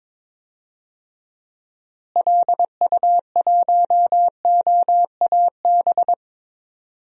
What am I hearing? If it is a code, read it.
Morse code, heard as a keyed tone, about 22 wpm, 700 Hz. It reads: LU1OAB
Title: DXCC-100 Call Signs - 6 Characters - Repeated 3x - 22wpm